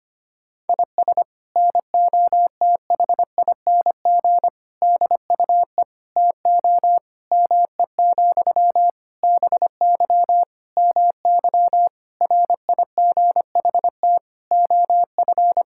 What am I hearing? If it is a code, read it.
Morse code, heard as a keyed tone, about 25 wpm, 700 Hz. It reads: IS NOTHING DUE TO ME, BY MY RIGHT OF
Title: Sentences from Top 1000 Words - Review Part 6 - 25wpm